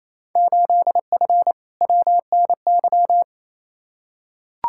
Morse code, heard as a keyed tone, about 28 wpm, 700 Hz. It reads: 8F WNY
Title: Field Day Exchanges Encore - 28wpm